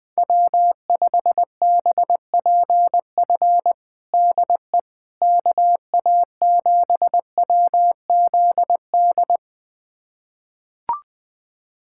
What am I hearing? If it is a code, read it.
Morse code, heard as a keyed tone, about 21 wpm, 700 Hz. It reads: W5BPF DE KA7WZD
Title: USA Callsign DE Callsign 20wpm